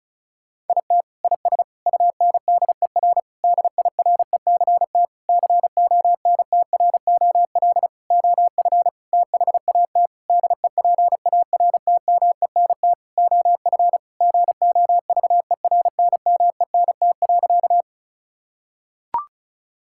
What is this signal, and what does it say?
Morse code, heard as a keyed tone, about 34 wpm, 700 Hz. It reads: IT IS UNDER DIRECT CONTROL OF THAT DEPARTMENT OF GOVERNMENT.
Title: Sentences from Top 1000 Words - Review Part 5 - 35wpm